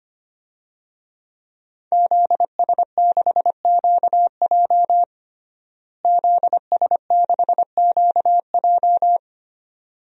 Morse code, heard as a keyed tone, about 25 wpm, 700 Hz. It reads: ZS6QJ ZS6QJ
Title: DXCC-100 Call Signs - 5 Characters - Repeated 2x - 25wpm